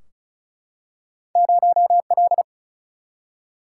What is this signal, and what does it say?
Morse code, heard as a keyed tone, about 36 wpm, 700 Hz. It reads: 0L
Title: Single-Letter Single-Number 35wpm